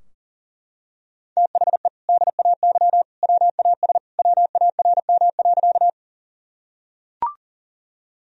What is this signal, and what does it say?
Morse code, heard as a keyed tone, about 37 wpm, 700 Hz. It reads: THE DAY WAS WARM.
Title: Sentences from Top 1000 Words - Review Part 2 - 40wpm